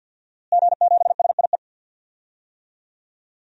Morse code, heard as a keyed tone, about 49 wpm, 700 Hz. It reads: G7SIE
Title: DXCC-100 Call Signs - 5 Characters - Repeated 2x - 50wpm